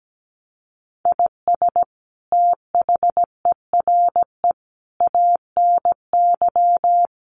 Morse code, heard as a keyed tone, about 17 wpm, 700 Hz. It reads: IS THERE ANY
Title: Sentences from Top 1200 Words 17wpm